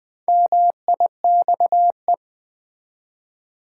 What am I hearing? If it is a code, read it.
Morse code, heard as a keyed tone, about 20 wpm, 700 Hz. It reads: MIXE
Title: Build A Word - 1 Letter At A Time - 3 to 5-Letter Words - 20wpm